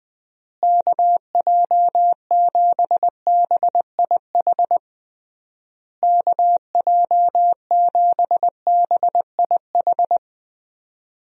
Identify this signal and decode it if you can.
Morse code, heard as a keyed tone, about 20 wpm, 700 Hz. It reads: KJ7BIH KJ7BIH
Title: USA Call Signs - 6 Characters - Repeated 3x - 20wpm